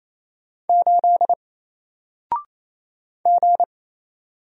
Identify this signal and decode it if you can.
Morse code, heard as a keyed tone, about 28 wpm, 700 Hz. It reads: 8 G
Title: Single Character (Letter:Number:Punctuation:Prosign) 28wpm